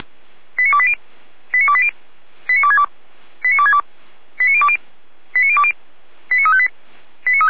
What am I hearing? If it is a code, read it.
Five-tone selective call (ZVEI), heard as quick runs of five tones, tones of about 70 ms. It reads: 89280 89280 89262 89262 89020 89020 89358
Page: /455/SDR/ZVEI/89320/2025.02.26/